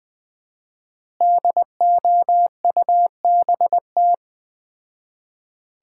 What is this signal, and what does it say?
Morse code, heard as a keyed tone, about 20 wpm, 700 Hz. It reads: DOUBT
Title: 4x Word-Spacing - Sets of 5 Words - 20wpm